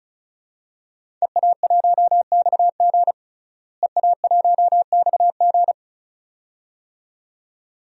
Morse code, heard as a keyed tone, about 34 wpm, 700 Hz. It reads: EA1XG EA1XG
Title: DXCC-100 Call Signs - 5 Characters - Repeated 2x - 35wpm